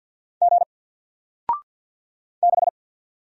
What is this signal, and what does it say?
Morse code, heard as a keyed tone, about 50 wpm, 700 Hz. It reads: G 6
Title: Single Letter-Number 50wpm